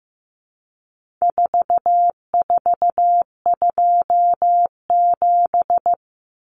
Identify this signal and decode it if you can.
Morse code, heard as a keyed tone, about 15 wpm, 700 Hz. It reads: 4427
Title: Numbers - 4-Digit 15wpm